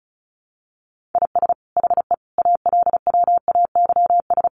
Morse code, heard as a keyed tone, about 34 wpm, 700 Hz. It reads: IS HE ALWAYS
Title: Sentences from Top 1400 Words 35wpm